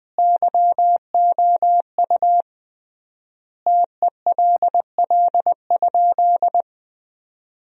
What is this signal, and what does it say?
Morse code, heard as a keyed tone, about 21 wpm, 700 Hz. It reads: YOU TELL?
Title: 3x Word-Spacing - Sentences from Top 200 Words - 20wpm